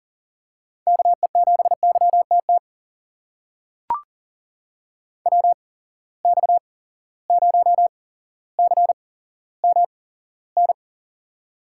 Morse code, heard as a keyed tone, about 40 wpm, spaced as for 17 wpm, 700 Hz. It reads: KE7YTT WX0CMN
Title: USA Call Signs - 6 Characters - 8x Character Spacing - 40wpm